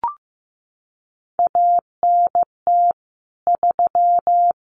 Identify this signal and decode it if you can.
Morse code, heard as a keyed tone, about 15 wpm, 700 Hz. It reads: ANT 3
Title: Common QSO Phrases 15wpm